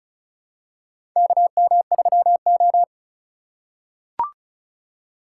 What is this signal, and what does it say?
Morse code, heard as a keyed tone, about 36 wpm, 700 Hz. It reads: KM3O
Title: USA Call Signs - 4 Characters - 3x Character Spacing - 35wpm